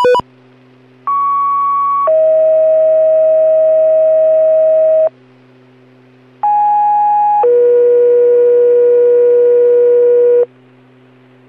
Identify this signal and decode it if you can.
Two-tone page, heard as a one-second tone, then a three-second tone, about 1120 then 640 Hz, about 830 then 480 Hz. It is heard from 1.1 to 10.5 s.